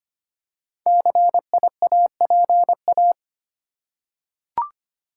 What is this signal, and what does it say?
Morse code, heard as a keyed tone, about 25 wpm, 700 Hz. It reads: CIAPA
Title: Code Groups - 5 Letters - 25wpm